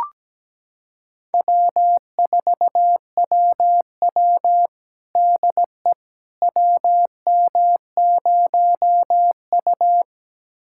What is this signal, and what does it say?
Morse code, heard as a keyed tone, about 17 wpm, 700 Hz. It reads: W4WW DE WM0U
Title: USA Callsign DE Callsign - 4 Characters - 17wpm